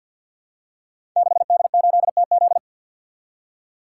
Morse code, heard as a keyed tone, about 47 wpm, 700 Hz. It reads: 6D8TZ
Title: Code Groups - 5 Letter:Numbers - 50wpm